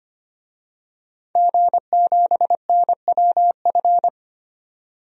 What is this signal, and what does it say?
Morse code, heard as a keyed tone, about 25 wpm, 700 Hz. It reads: G7NWF
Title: DXCC-100 Call Signs - 5 Characters - 25wpm